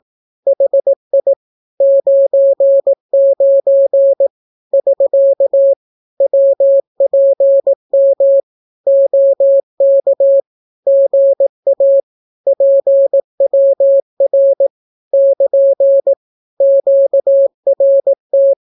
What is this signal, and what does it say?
Morse code, heard as a keyed tone, about 18 wpm, 550 Hz. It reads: HI 99 <SK> WPM OK GA PWR <KN> QRT
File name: naukapodstaw / skroty / skroty